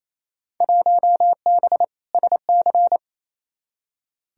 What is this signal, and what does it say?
Morse code, heard as a keyed tone, about 28 wpm, 700 Hz. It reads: 1B SC
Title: Field Day Exchanges Encore - 28wpm